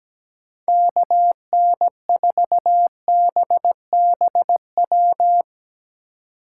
Morse code, heard as a keyed tone, about 21 wpm, 700 Hz. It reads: KN4BBW
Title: Speed Racing - USA 2x3 Call Signs 17wpm